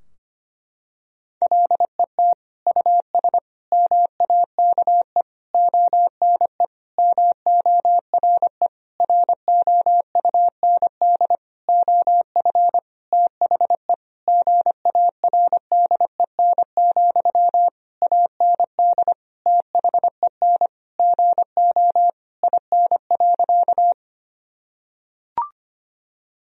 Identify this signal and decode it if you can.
Morse code, heard as a keyed tone, about 25 wpm, 700 Hz. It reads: LET US MAKE ONE MORE ROUND OF THE GARDEN, AND THEN GO IN.
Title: Sentences from Top 1000 Words - Review Part 4 - 25wpm